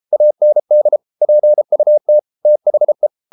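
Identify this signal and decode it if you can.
Morse code, heard as a keyed tone, about 33 wpm, 600 Hz. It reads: AND PUT THE
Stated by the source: Morse code practice at 32 wpm from CW med Gnister, reading Ken Follett's novel Never in English.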